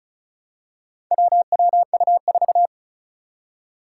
Morse code, heard as a keyed tone, about 36 wpm, 700 Hz. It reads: WWU4
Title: Four Challenging Letter-Number Combinations 35wpm